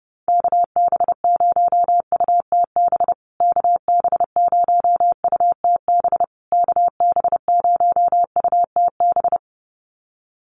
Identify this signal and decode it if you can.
Morse code, heard as a keyed tone, about 30 wpm, 700 Hz. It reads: KB0UTB KB0UTB KB0UTB
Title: USA Call Signs Repeated 3x 30wpm